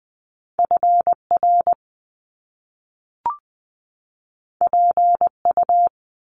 Morse code, heard as a keyed tone, about 20 wpm, 700 Hz. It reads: F R P U
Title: Top Two Letter Combinations in Words 20wpm